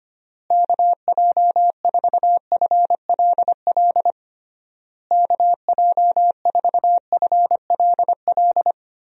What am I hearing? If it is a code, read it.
Morse code, heard as a keyed tone, about 25 wpm, 700 Hz. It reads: KJ4FLL KJ4FLL
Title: USA Call Signs - 6 Characters - Repeated 3x - 25wpm